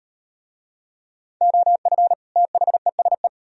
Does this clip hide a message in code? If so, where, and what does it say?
Yes: Morse code from 1.4 to 3.3 s, heard as a keyed tone, about 36 wpm, 700 Hz. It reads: OF THESE